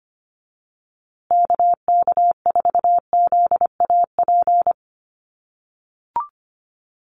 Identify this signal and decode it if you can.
Morse code, heard as a keyed tone, about 25 wpm, 700 Hz. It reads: KK4ZAP
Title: USA Call Signs Repeated 2x 25wpm